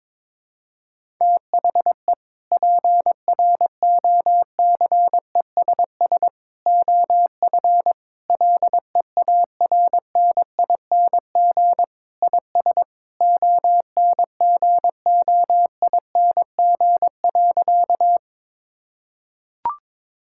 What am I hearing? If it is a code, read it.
Morse code, heard as a keyed tone, about 22 wpm, 700 Hz. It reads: THE PROCESS OF LEARNING IS ONGOING.
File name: practice / 22wpm / Sentences with 3-Word Phrases - 22wpm